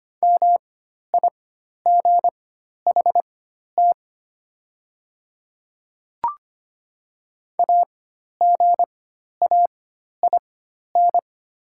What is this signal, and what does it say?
Morse code, heard as a keyed tone, about 25 wpm, spaced as for 12 wpm, 700 Hz. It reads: MIGHT AGAIN
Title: 4x Character Spacing - Top 500 Words - 25wpm